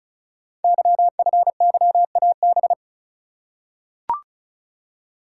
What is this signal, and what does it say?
Morse code, heard as a keyed tone, about 36 wpm, 700 Hz. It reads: YFYAB
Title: Code Groups - 5 Letter:Numbers - 35wpm